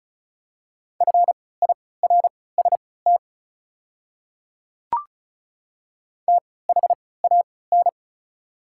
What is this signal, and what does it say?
Morse code, heard as a keyed tone, about 36 wpm, spaced as for 20 wpm, 690 Hz. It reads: FIRST THAN
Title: 3x Character Spacing - Top 100 Words - 35wpm